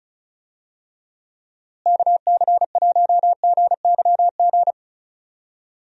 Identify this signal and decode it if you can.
Morse code, heard as a keyed tone, about 36 wpm, 700 Hz. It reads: KC1GYG
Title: USA Call Signs - 6 Characters - Repeated 2x - 35wpm